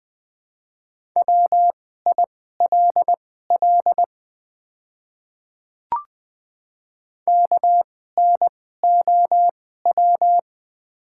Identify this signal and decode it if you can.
Morse code, heard as a keyed tone, about 20 wpm, spaced as for 14 wpm, 690 Hz. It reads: WILL KNOW
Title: 2x Character Spacing - Top 100 Words - 20wpm